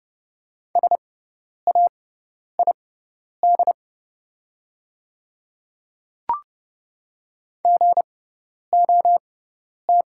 Morse code, heard as a keyed tone, about 30 wpm, spaced as for 10 wpm, 700 Hz. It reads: SAID GOT
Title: 6x Character Spacing - Top 300 Words - 30wpm